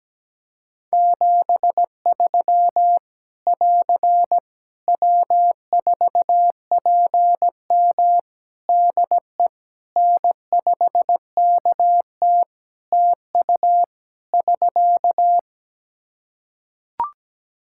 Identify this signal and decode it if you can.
Morse code, heard as a keyed tone, about 17 wpm, 700 Hz. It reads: 73 <AR> W4PM DE N5KT TU <SK>
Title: K1USN SST QSO Practice 17wpm